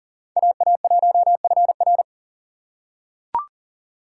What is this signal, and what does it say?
Morse code, heard as a keyed tone, about 40 wpm, 700 Hz. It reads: AA1FR
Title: USA Call Signs - 5 Characters - 2x Character Spacing - 40wpm